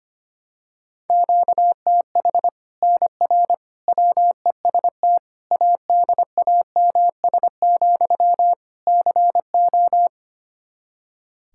Morse code, heard as a keyed tone, about 25 wpm, 700 Hz. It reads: QTH NR WEST ADAMS, CO
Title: Practice QSO - Exchange 1 - Beginning - 25wpm